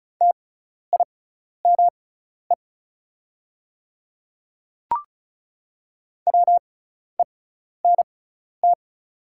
Morse code, heard as a keyed tone, about 36 wpm, spaced as for 12 wpm, 690 Hz. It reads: TIME WENT
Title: 6x Character Spacing - Top 300 Words - 35wpm